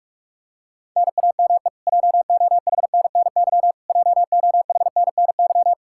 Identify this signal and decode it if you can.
Morse code, heard as a keyed tone, about 47 wpm, 700 Hz. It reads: NAME JOHNNY JOHNNY
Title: Practice QSO - Exchange 1 - Beginning - 45wpm